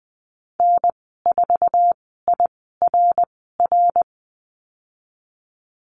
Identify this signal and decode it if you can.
Morse code, heard as a keyed tone, about 20 wpm, spaced as for 15 wpm, 700 Hz. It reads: N4IRR
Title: 062 Lesson 11g - Letter h - Call Signs 20wpm